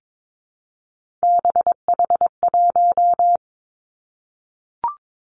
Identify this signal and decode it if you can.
Morse code, heard as a keyed tone, about 22 wpm, 700 Hz. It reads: BH1
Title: Three Challenging Letter-Number Combinations 22wpm